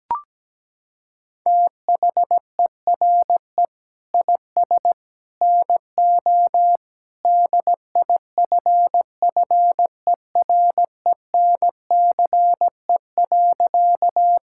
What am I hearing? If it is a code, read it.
Morse code, heard as a keyed tone, about 17 wpm, 700 Hz. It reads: THERE IS NO DIFFERENCE.
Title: Sentences from Top 1000 Words - Review Part 5 - 17wpm